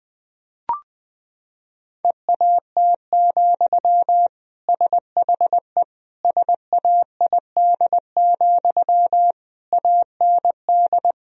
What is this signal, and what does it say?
Morse code, heard as a keyed tone, about 21 wpm, 700 Hz. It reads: EAT, SHE SAID, AND
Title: Sentences from Top 1000 Words - Review Part 2 - 20wpm